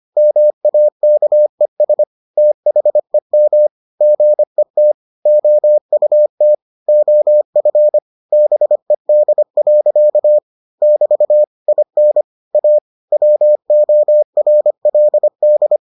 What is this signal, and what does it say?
Morse code, heard as a keyed tone, about 25 wpm, 600 Hz. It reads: MAKES THEM GET OUT OF BED. <BT> IN A WORLD